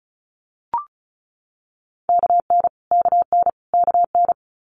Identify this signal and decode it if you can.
Morse code, heard as a keyed tone, about 36 wpm, 700 Hz. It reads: KN KN KN
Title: Multi-letter Phonemes Repeated 3x 35wpm